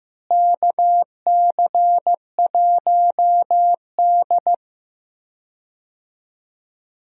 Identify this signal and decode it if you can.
Morse code, heard as a keyed tone, about 15 wpm, 700 Hz. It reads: KC1D
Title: USA Call Signs - 4 Characters - Repeated 2x - 15wpm